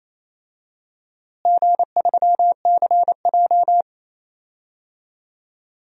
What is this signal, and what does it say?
Morse code, heard as a keyed tone, about 28 wpm, 700 Hz. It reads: G3CJ
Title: DXCC-100 Call Signs - 4 Characters - Repeated 3x - 28wpm